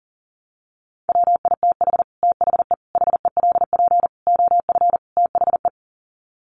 Morse code, heard as a keyed tone, about 40 wpm, 700 Hz. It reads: WITH THE HELP OF THE
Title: Sets of 5 Words Encore 40wpm